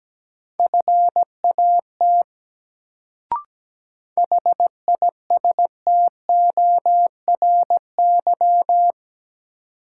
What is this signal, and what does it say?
Morse code, heard as a keyed tone, about 17 wpm, 700 Hz. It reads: FAT HISTORY
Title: Sentences from Top 600 Words - New Word Drill - 17wpm